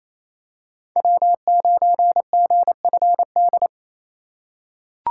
Morse code, heard as a keyed tone, about 28 wpm, 700 Hz. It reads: W9GFD
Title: USA Call Signs - 5 Characters - 2x Character Spacing - 28wpm